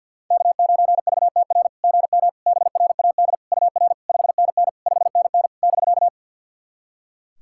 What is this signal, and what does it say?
Morse code, heard as a keyed tone, about 47 wpm, 690 Hz. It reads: K9VTR GM BRAD UR 5NN 5NN <BK>
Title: SOTA QSO Practice 50wpm